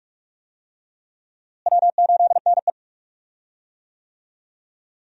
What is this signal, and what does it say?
Morse code, heard as a keyed tone, about 42 wpm, 700 Hz. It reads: W8NE
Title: USA Call Signs - 4 Characters - Repeated 3x - 45wpm